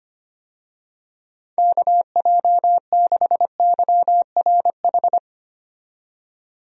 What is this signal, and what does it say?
Morse code, heard as a keyed tone, about 25 wpm, 700 Hz. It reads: KJ6YRH
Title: USA Call Signs - 6 Characters - 25wpm